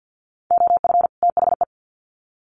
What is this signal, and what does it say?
Morse code, heard as a keyed tone, about 50 wpm, 700 Hz. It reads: OF THE
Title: Sets of 2 Words 50wpm